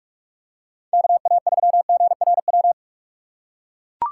Morse code, heard as a keyed tone, about 44 wpm, 700 Hz. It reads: KA3GRW